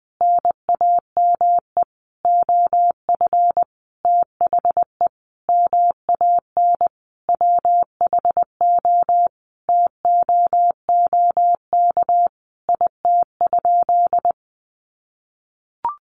Morse code, heard as a keyed tone, about 20 wpm, 700 Hz. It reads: NAME OF THE MAN WHO TOOK IT?
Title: Sentences from Top 300 Words 20wpm